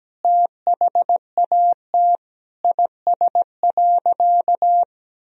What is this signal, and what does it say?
Morse code, heard as a keyed tone, about 17 wpm, 700 Hz. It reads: THAT IS.
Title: Sentences from Top 1000 Words - Review Part 4 - 17wpm